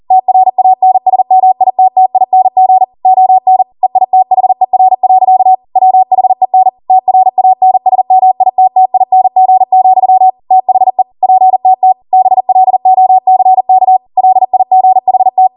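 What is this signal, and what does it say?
Morse code, heard as a keyed tone, about 40 wpm, 750 Hz. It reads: TRANSMITTING ON EITHER. WHEN TRANSMITTING, THE PTT BLOCK LIGHT
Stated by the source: Code practice at 40 wpm.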